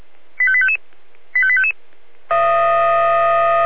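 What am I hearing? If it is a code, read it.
Five-tone selective call (ZVEI), heard as quick runs of five tones, tones of about 70 ms. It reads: 85855 85855
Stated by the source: Five-tone ZVEI selective call received by SDR.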